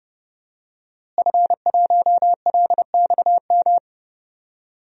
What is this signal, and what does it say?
Morse code, heard as a keyed tone, about 30 wpm, 700 Hz. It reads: F1LXM
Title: DXCC-100 Call Signs - 5 Characters - Repeated 2x - 30wpm